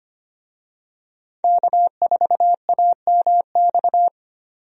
Morse code, heard as a keyed tone, about 25 wpm, 700 Hz. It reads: K4AMX
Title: USA Call Signs - 5 Characters - 25wpm